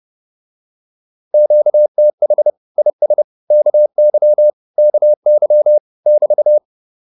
Morse code, heard as a keyed tone, about 30 wpm, 600 Hz. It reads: QTH IS KY KY <BT>
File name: QSO109 30